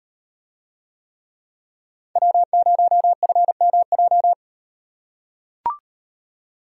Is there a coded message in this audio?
Yes: Morse code from 2.2 to 4.3 s, heard as a keyed tone, about 36 wpm, 700 Hz. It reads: W0FMJ